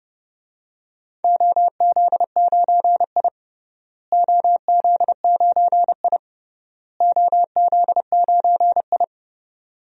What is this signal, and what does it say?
Morse code, heard as a keyed tone, about 30 wpm, 700 Hz. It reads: OZ9I OZ9I OZ9I
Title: DXCC-100 Call Signs - 4 Characters - Repeated 3x - 30wpm